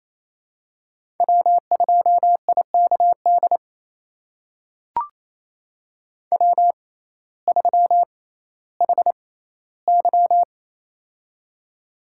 Morse code, heard as a keyed tone, about 28 wpm, spaced as for 15 wpm, 700 Hz. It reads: W2IKD W3HY
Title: USA Call Signs - 6x Character Spacing - 28wpm